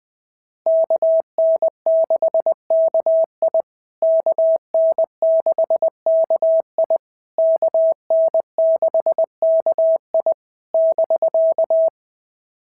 Morse code, heard as a keyed tone, about 20 wpm, 650 Hz. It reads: KN6KI KN6KI KN6KI <BK>
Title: POTA QSO Practice - 20wpm